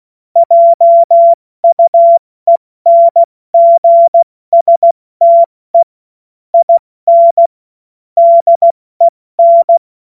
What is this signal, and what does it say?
Morse code, heard as a keyed tone, about 16 wpm, 680 Hz. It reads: JUENGSTE IN DEN
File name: Witz-08-Juli-2025-70-F80